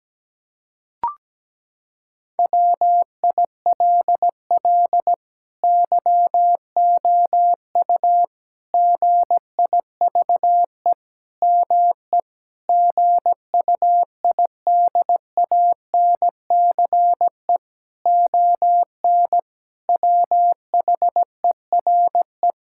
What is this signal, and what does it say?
Morse code, heard as a keyed tone, about 17 wpm, 700 Hz. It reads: WILL YOU GIVE ME GUIDANCE ON WHERE
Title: Sentences from Top 1600 Words 17wpm